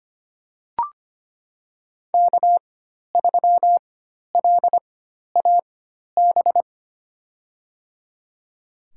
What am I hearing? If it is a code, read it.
Morse code, heard as a keyed tone, about 25 wpm, spaced as for 12 wpm, 700 Hz. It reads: K3LAB
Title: USA Call Signs - 5 Characters - 4x Character Spacing - 25wpm